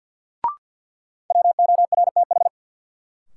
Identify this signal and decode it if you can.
Morse code, heard as a keyed tone, about 53 wpm, 700 Hz. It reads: WORTH
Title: Sentences from Top 800 Words - New Word Drill - 50wpm